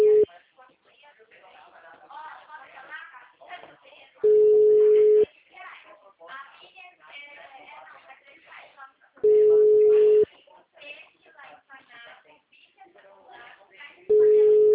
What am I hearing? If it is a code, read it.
Ringback tone (Europe, 425 Hz), with speech.